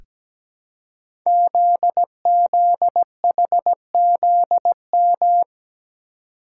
Morse code, heard as a keyed tone, about 17 wpm, 700 Hz. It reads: ZZHZM
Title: Code Groups - 5 Letters - 17wpm